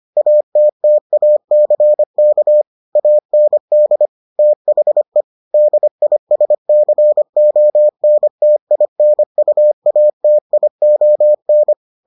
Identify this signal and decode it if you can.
Morse code, heard as a keyed tone, about 25 wpm, 600 Hz. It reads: ATTACK AND THE DISCONTINUATION